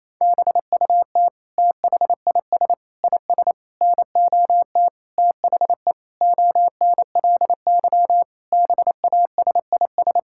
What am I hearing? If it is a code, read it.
Morse code, heard as a keyed tone, about 28 wpm, 700 Hz. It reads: BUT THIS IS NOT THE ONLY BASIS
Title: Sentences from Top 1600 Words 28wpm